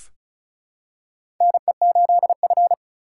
Morse code, heard as a keyed tone, about 36 wpm, 700 Hz. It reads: NE8F
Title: USA Call Signs - 4 Characters - 6x Character Spacing - 35wpm